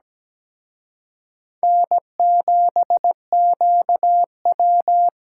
Morse code, heard as a keyed tone, about 20 wpm, 700 Hz. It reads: N7QW
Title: Speed Racing - USA Call Signs 17wpm